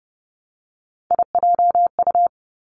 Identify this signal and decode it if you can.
Morse code, heard as a keyed tone, about 30 wpm, 700 Hz. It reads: IJU
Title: Three Challenging Letter-Number Combinations 30wpm